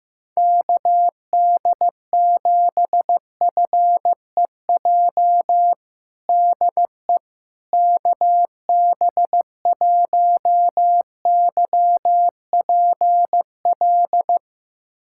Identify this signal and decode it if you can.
Morse code, heard as a keyed tone, about 15 wpm, 700 Hz. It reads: KD7FEJ DE KB1YPL
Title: USA Callsign DE Callsign - 6 Characters - 15wpm